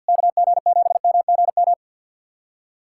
Morse code, heard as a keyed tone, about 51 wpm, 700 Hz. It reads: KG7MGM
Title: USA Callsign DE Callsign 50wpm